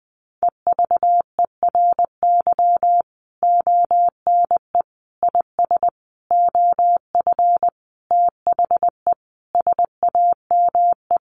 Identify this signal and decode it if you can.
Morse code, heard as a keyed tone, about 20 wpm, 700 Hz. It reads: EVERY ONE IS OF THE SAME
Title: Sentences from Top 1200 Words 20wpm